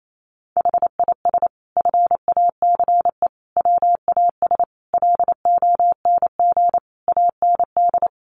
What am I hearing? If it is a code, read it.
Morse code, heard as a keyed tone, about 28 wpm, 700 Hz. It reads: HIS FACE WAS LONG AND
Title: Sentences from Top 1200 Words 28wpm